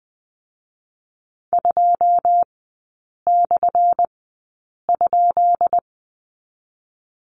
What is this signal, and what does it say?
Morse code, heard as a keyed tone, about 20 wpm, spaced as for 8 wpm, 700 Hz. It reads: 2/?
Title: 184 Lesson 30e - Letter b - Three Characters 20wpm